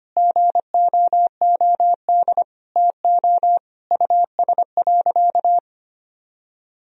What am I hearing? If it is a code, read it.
Morse code, heard as a keyed tone, about 25 wpm, 700 Hz. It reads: GOOD TO US.
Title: Sentences from Top 1000 Words - Review Part 3 - 25wpm